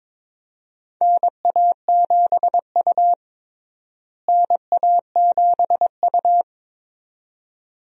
Morse code, heard as a keyed tone, about 21 wpm, 700 Hz. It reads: NA7U NA7U
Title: USA Call Signs - 4 Characters - Repeated 2x - 22wpm